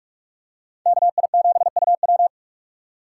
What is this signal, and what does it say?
Morse code, heard as a keyed tone, about 45 wpm, 700 Hz. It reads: KI7UW
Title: USA Call Signs - 5 Characters - 4x Character Spacing - 45wpm